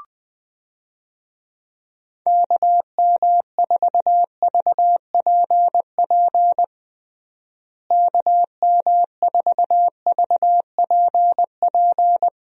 Morse code, heard as a keyed tone, about 20 wpm, 700 Hz. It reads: KM4VPP KM4VPP
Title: USA Call Signs - 6 Characters - Repeated 3x - 20wpm